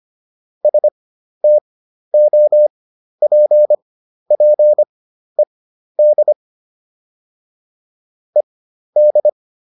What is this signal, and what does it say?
Morse code, heard as a keyed tone, about 25 wpm, spaced as for 12 wpm, 600 Hz. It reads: STOPPED ED